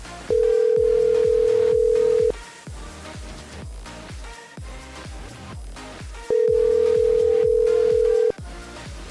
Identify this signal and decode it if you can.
Ringback tone (North America), 440 + 480 Hz.